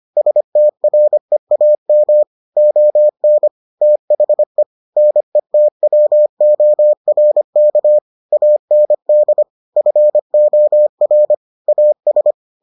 Morse code, heard as a keyed tone, about 25 wpm, 600 Hz. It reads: STREAM ON THE NETWORK AND FOR AS